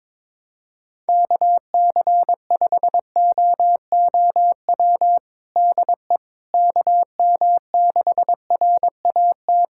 Morse code, heard as a keyed tone, about 21 wpm, 700 Hz. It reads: KC5OOW DE KM6RAT
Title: USA Callsign DE Callsign - 6 Characters - 22wpm